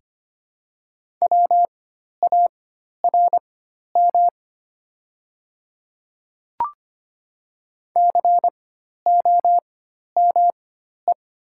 Morse code, heard as a keyed tone, about 25 wpm, spaced as for 12 wpm, 690 Hz. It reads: WARM COME
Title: 4x Character Spacing - Top 500 Words - 25wpm